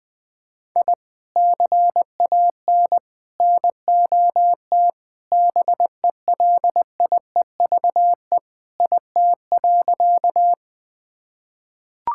Morse code, heard as a keyed tone, about 20 wpm, 700 Hz. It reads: I CAN NOT BELIEVE IT.
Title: Sentences with 4-Word Phrases - 20wpm